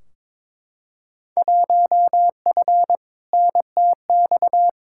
Morse code, heard as a keyed tone, about 22 wpm, 700 Hz. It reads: 1F NTX
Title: Field Day Exchanges Encore - 22wpm